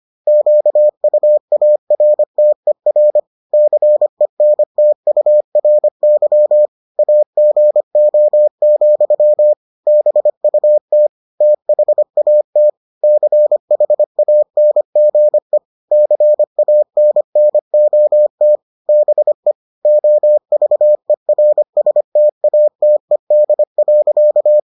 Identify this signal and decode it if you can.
Morse code, heard as a keyed tone, about 25 wpm, 600 Hz. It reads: QUARTER CENTURY AGO, BUT THAT CHANGE CANNOT BE OVERSTATED.